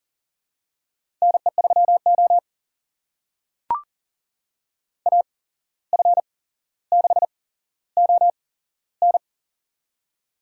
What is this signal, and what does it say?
Morse code, heard as a keyed tone, about 40 wpm, spaced as for 16 wpm, 700 Hz. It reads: NE3O AF6ON
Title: USA Call Signs - 8x Character Spacing - 40wpm